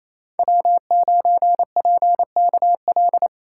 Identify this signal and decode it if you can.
Morse code, heard as a keyed tone, about 28 wpm, 700 Hz. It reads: W9PKL
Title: USA Call Signs - 5 Characters - 4x Character Spacing - 28wpm